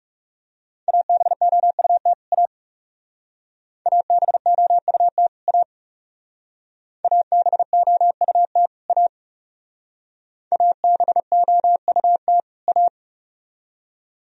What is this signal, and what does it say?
Morse code, heard as a keyed tone, about 37 wpm, 700 Hz. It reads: ABOUT A ABOUT A ABOUT A ABOUT A
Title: Speed-Racing - Sets of 2 Words 30wpm